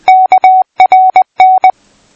Morse code, heard as a keyed tone, about 20 wpm, 750 Hz. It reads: KRN
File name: pi4aa / PI4AA Uitzending